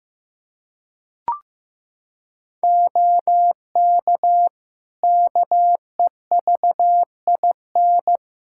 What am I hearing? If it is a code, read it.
Morse code, heard as a keyed tone, about 15 wpm, 700 Hz. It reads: OK KEVIN
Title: Practice QSO - Exchange 3 - More Info - 15wpm